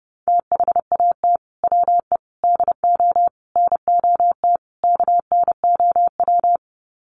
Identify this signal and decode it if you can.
Morse code, heard as a keyed tone, about 28 wpm, 700 Hz. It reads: THAT WE DO NOT KNOW
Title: Sets of 5 Words Encore2 30wpm